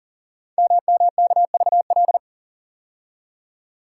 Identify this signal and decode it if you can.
Morse code, heard as a keyed tone, about 42 wpm, 690 Hz. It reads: MMKVL
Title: Code Groups - 5 Letters - 40wpm